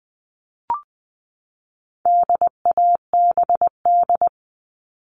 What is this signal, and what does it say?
Morse code, heard as a keyed tone, about 20 wpm, 700 Hz. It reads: DABD
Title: All Cut Numbers - 4-Digit 20wpm